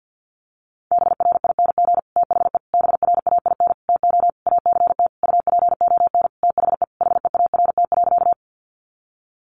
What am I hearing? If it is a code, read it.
Morse code, heard as a keyed tone, about 50 wpm, 700 Hz. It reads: BRING THE BRAIN TO ACT UPON THE HEART.
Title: Sentences from Top 1200 Words 50wpm